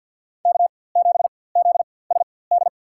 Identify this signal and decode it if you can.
Morse code, heard as a keyed tone, about 51 wpm, spaced as for 23 wpm, 700 Hz. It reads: K7ZSD
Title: USA Call Signs - 5 Characters - 4x Character Spacing - 50wpm